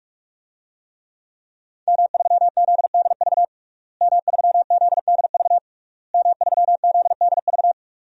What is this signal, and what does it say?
Morse code, heard as a keyed tone, about 42 wpm, 700 Hz. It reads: M3ZDV M3ZDV M3ZDV
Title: DXCC-100 Call Signs - 5 Characters - Repeated 3x - 45wpm